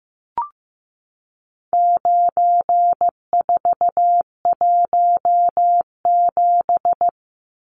Morse code, heard as a keyed tone, about 15 wpm, 700 Hz. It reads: 9417
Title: Numbers - 4-Digit 15wpm